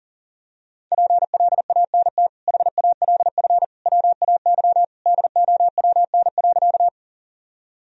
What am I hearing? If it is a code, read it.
Morse code, heard as a keyed tone, about 40 wpm, 700 Hz. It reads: PLANT HALF WAY DOWN.
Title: Sentences from Top 1000 Words - Review Part 1 - 40wpm